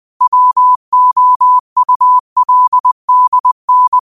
Morse code, heard as a keyed tone, about 20 wpm, 1000 Hz. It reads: WOULDN
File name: qotdfiles / 20wpm